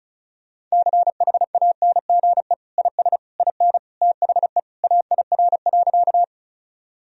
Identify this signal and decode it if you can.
Morse code, heard as a keyed tone, about 34 wpm, 700 Hz. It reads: CHANGE IS IN THE AIR.
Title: Sentences from Top 1000 Words - Review Part 1 - 35wpm